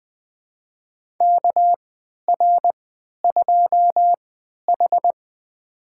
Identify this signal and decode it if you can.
Morse code, heard as a keyed tone, about 20 wpm, spaced as for 11 wpm, 700 Hz. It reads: KR2H
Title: USA Call Signs - 4 Characters - 3x Character Spacing - 20wpm